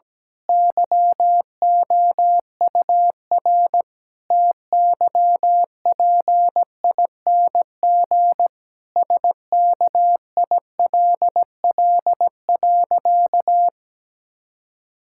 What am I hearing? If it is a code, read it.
Morse code, heard as a keyed tone, about 17 wpm, 700 Hz. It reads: YOUR TYPING SKILL.
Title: Pangram Challenge v2 - 17wpm